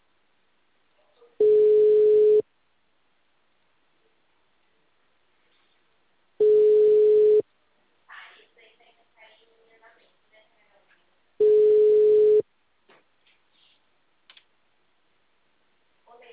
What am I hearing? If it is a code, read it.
Ringback tone (Europe, 425 Hz), with speech.